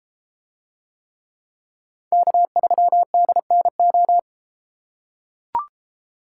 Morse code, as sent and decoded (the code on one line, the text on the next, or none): -.- ...-- -.. -. ---
K3DNO